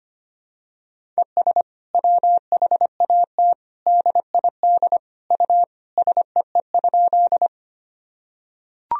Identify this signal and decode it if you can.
Morse code, heard as a keyed tone, about 25 wpm, 700 Hz. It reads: ES WHAT DID U SEE?
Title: Sentences from Top 100 Words and Abbreviations 25wpm